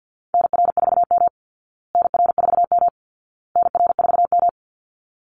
Morse code, heard as a keyed tone, about 50 wpm, 700 Hz. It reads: NR4M NR4M NR4M
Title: Call Signs Repeated 2x - 50wpm